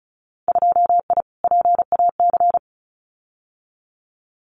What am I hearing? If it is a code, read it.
Morse code, heard as a keyed tone, about 36 wpm, 700 Hz. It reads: 2I PAC
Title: Winter Field Day Exchanges Encore 35wpm